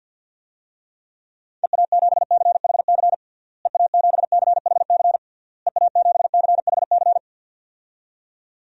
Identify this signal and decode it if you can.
Morse code, heard as a keyed tone, about 51 wpm, 700 Hz. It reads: EA7CHC EA7CHC EA7CHC
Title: DXCC-100 Call Signs - 6 Characters - Repeated 3x - 50wpm